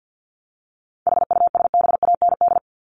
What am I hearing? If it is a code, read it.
Morse code, heard as a keyed tone, about 50 wpm, 700 Hz. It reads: HUSBAND
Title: Top 1501-1750 Words 50wpm